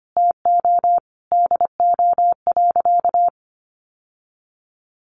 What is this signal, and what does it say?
Morse code, heard as a keyed tone, about 25 wpm, 700 Hz. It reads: TO DO.
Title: Sentences from Top 600 Words 25wpm